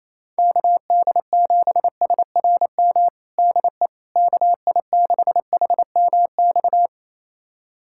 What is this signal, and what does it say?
Morse code, heard as a keyed tone, about 28 wpm, 700 Hz. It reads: KD7SRM DE KI6HMX
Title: USA Callsign DE Callsign 28wpm